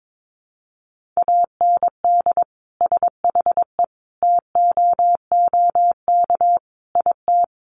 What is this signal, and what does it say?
Morse code, heard as a keyed tone, about 22 wpm, 700 Hz. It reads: AND SHE TOOK IT
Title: Sentences from Top 1400 Words 22wpm